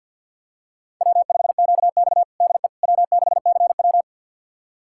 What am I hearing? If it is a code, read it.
Morse code, heard as a keyed tone, about 52 wpm, 700 Hz. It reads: W5QX DE W6CW
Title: USA Callsign DE Callsign - 4 Characters - 50wpm